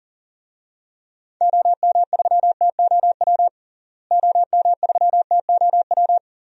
Morse code, heard as a keyed tone, about 40 wpm, 700 Hz. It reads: OM3TOW OM3TOW
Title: DXCC-100 Call Signs - 6 Characters - Repeated 2x - 40wpm